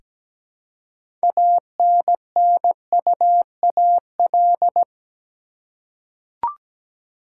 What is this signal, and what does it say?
Morse code, heard as a keyed tone, about 17 wpm, 700 Hz. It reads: ANNUAL
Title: Six Letter Words - 17wpm